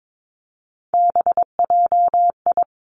Morse code, heard as a keyed tone, about 22 wpm, 700 Hz. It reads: BJI
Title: Three Challenging Letter-Number Combinations 22wpm